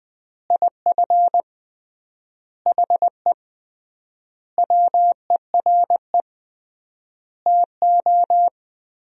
Morse code, heard as a keyed tone, about 19 wpm, 700 Hz. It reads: IF HE WERE TO
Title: 3x Word-Spacing - Sentences from Top 400 Words - 20wpm